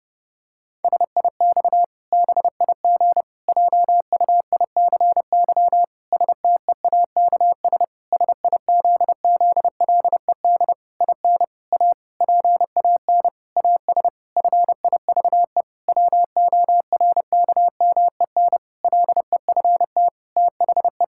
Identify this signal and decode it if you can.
Morse code, heard as a keyed tone, about 30 wpm, 700 Hz. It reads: SIX BIG JUICY STEAKS SIZZLED IN A PAN AS FIVE WORKMEN LEFT THE
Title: Pangram Challenge v2 - 30wpm